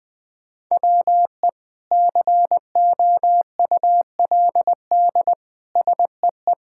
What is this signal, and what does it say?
Morse code, heard as a keyed tone, about 20 wpm, 700 Hz. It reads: WE COULD SEE
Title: Sentences from Top 1000 Words - Review Part 5 - 20wpm